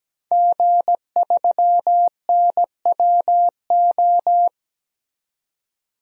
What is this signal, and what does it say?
Morse code, heard as a keyed tone, about 17 wpm, 700 Hz. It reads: G3NWO
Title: DXCC-100 Call Signs - 5 Characters - 17wpm